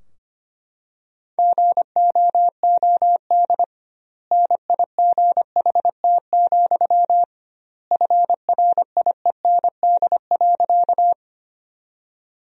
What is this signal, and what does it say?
Morse code, heard as a keyed tone, about 25 wpm, 700 Hz. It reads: GOOD NIGHT, FRIEND.
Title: 2x Word-Spacing - Sentences from Top 300 Words - 25wpm